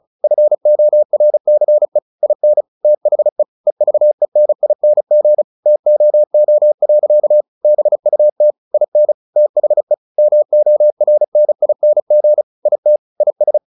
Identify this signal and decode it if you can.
Morse code, heard as a keyed tone, about 34 wpm, 600 Hz. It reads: FORCE IN THE EVENING TOO. BUT IN THE MORNING IT IS
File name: the old man and the sea / the old man and the sea 1 35wpm